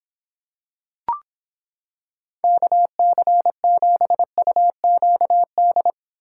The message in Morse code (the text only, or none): KC7UQD